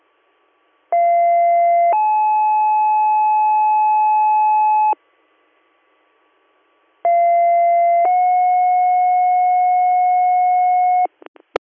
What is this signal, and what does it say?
Two-tone page, heard as a one-second tone, then a three-second tone, about 690 then 860 Hz, about 690 then 730 Hz. It is heard from 0.9 to 11.1 s.